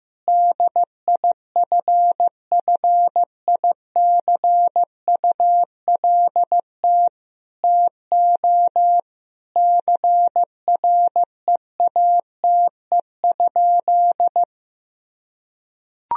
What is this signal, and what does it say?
Morse code, heard as a keyed tone, about 15 wpm, 700 Hz. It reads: DIFFICULT TO CREATE?
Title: Sentences from Top 1000 Words - Review Part 3 - 15wpm